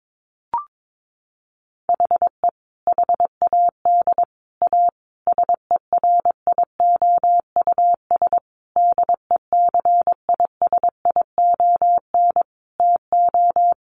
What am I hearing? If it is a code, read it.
Morse code, heard as a keyed tone, about 22 wpm, 700 Hz. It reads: HE HAD A SERIOUS DECISION TO
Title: Sentences from Top 1100 Words 22wpm